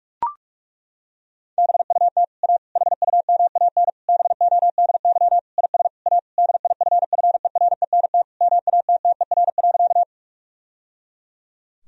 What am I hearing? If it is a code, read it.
Morse code, heard as a keyed tone, about 41 wpm, 700 Hz. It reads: BUT A HUMAN BODY IS A DIFFERENT MATTER.
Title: Sentences from Top 1000 Words - Review Part 3 - 45wpm